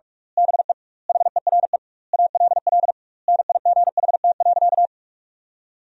Morse code, heard as a keyed tone, about 46 wpm, 700 Hz. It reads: BE HERE ALL NIGHT.
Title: 2x Word-Spacing - Sentences from Top 300 Words - 45wpm